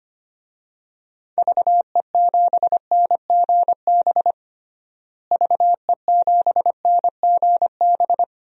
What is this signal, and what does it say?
Morse code, heard as a keyed tone, about 25 wpm, 700 Hz. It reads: VE7NGB VE7NGB
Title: DXCC-100 Call Signs - 6 Characters - Repeated 2x - 25wpm